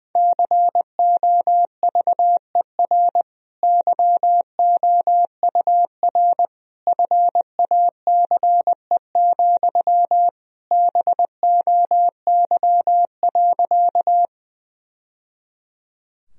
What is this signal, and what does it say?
Morse code, heard as a keyed tone, about 21 wpm, 700 Hz. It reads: COVER YOUR FACE, BOY.
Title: Sentences from Top 1000 Words - Review Part 1 - 20wpm